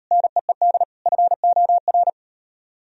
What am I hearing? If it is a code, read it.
Morse code, heard as a keyed tone, about 36 wpm, 700 Hz. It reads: NEED FOR